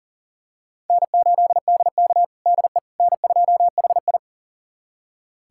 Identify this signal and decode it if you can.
Morse code, heard as a keyed tone, about 40 wpm, 700 Hz. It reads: N8DK DE N2HI
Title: USA Callsign DE Callsign - 4 Characters - 40wpm